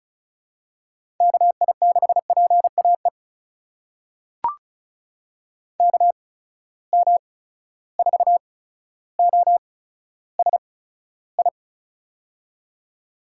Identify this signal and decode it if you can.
Morse code, heard as a keyed tone, about 36 wpm, spaced as for 15 wpm, 700 Hz. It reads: KI6PAE KM4OSI
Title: USA Call Signs - 6 Characters - 8x Character Spacing - 35wpm